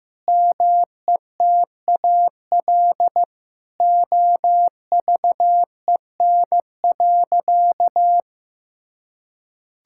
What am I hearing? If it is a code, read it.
Morse code, heard as a keyed tone, about 15 wpm, 700 Hz. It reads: METAL OVEN.
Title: Sentences from Top 1000 Words - Review Part 6 - 15wpm